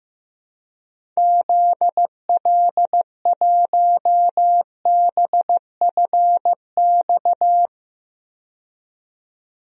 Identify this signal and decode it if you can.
Morse code, heard as a keyed tone, about 16 wpm, 700 Hz. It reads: ZL1BFX
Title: DXCC-100 Call Signs - 6 Characters - 15wpm